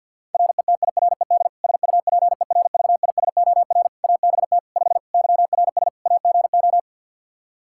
Morse code, heard as a keyed tone, about 48 wpm, 700 Hz. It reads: RETIRED SUPERVISOR ABT 5 YRS AGO
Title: Practice QSO - Exchange 3 - More Info - 50wpm